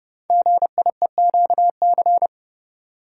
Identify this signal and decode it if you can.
Morse code, heard as a keyed tone, about 28 wpm, 700 Hz. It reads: GIEQC
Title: Code Groups - 5 Letter:Numbers - 30wpm